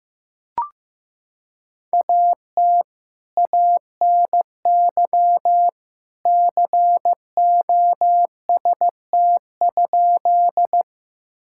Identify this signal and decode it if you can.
Morse code, heard as a keyed tone, about 15 wpm, 700 Hz. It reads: AT ANY COST?
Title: Sentences from Top 1000 Words - Review Part 5 - 15wpm